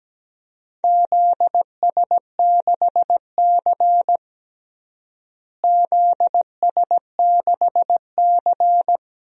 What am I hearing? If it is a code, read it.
Morse code, heard as a keyed tone, about 17 wpm, 700 Hz. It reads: ZS6C ZS6C
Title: DXCC-100 Call Signs - 4 Characters - Repeated 3x - 17wpm